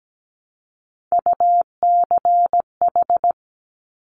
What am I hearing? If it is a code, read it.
Morse code, heard as a keyed tone, about 17 wpm, 700 Hz. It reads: UCH
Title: Top Three Letter Combinations in Words 17wpm